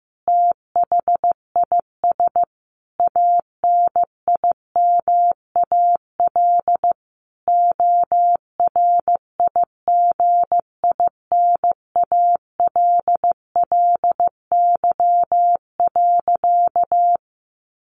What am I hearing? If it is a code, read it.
Morse code, heard as a keyed tone, about 15 wpm, 700 Hz. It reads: THIS ANIMAL ORIGINALLY.
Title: A Sound Of Thunder 15wpm